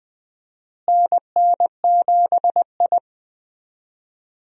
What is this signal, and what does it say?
Morse code, heard as a keyed tone, about 20 wpm, 700 Hz. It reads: NN7I
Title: USA Call Signs - 4 Characters - Repeated 2x - 20wpm